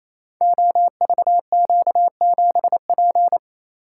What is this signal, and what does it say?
Morse code, heard as a keyed tone, about 27 wpm, 700 Hz. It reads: OVQ7P
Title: Code Groups - 5 Letter:Numbers - 28wpm